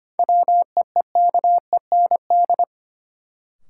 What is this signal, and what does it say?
Morse code, heard as a keyed tone, about 25 wpm, 700 Hz. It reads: WEEKEND
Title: Sets of 5 Words - New Word Drill - 25wpm